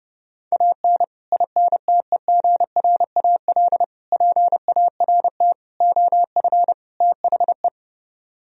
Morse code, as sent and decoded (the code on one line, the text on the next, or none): .- -. / .. -. - . --. .-. .- .-.. / .--. .- .-. - / --- ..-. / - .... .
AN INTEGRAL PART OF THE